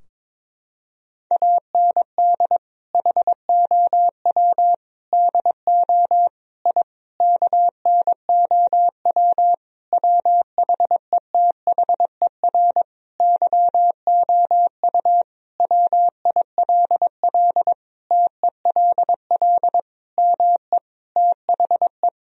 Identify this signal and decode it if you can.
Morse code, heard as a keyed tone, about 22 wpm, 690 Hz. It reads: AND HOW DO I KNOW WHETHER YOU WILL TELL ME THE
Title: Sentences from Top 1000 Words - Review Part 3 - 22wpm